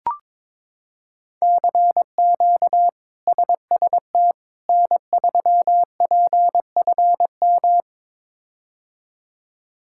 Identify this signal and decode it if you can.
Morse code, heard as a keyed tone, about 21 wpm, 700 Hz. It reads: CQ SST N3PFM
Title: K1USN SST QSO Practice - 22wpm